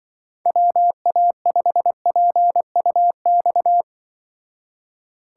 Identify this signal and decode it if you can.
Morse code, heard as a keyed tone, about 23 wpm, 700 Hz. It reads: WA5PUX
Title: Speed Racing - USA 2x3 Call Signs 20wpm